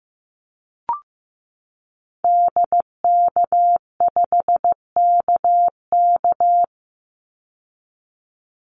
Morse code, heard as a keyed tone, about 15 wpm, 700 Hz. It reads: DK5KK
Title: Call Signs Repeated 2x - 15wpm